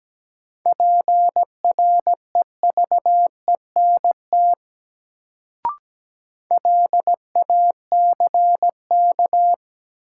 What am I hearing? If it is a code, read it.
Morse code, heard as a keyed tone, about 17 wpm, 700 Hz. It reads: PREVENT LACK
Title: Sentences from Top 1100 Words - New Word Drill Alt - 17wpm